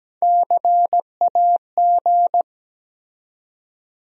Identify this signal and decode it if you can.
Morse code, heard as a keyed tone, about 17 wpm, 700 Hz. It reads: CAG
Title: USA Call Sign Postfixes - 3 Letters 17wpm